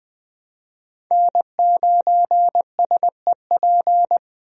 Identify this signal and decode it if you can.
Morse code, heard as a keyed tone, about 20 wpm, 700 Hz. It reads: N9SEP
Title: USA Call Signs - 5 Characters - Repeated 2x - 20wpm